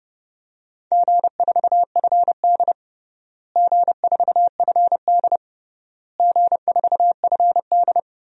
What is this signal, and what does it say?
Morse code, heard as a keyed tone, about 30 wpm, 700 Hz. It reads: G4FD G4FD G4FD
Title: DXCC-100 Call Signs - 4 Characters - Repeated 3x - 30wpm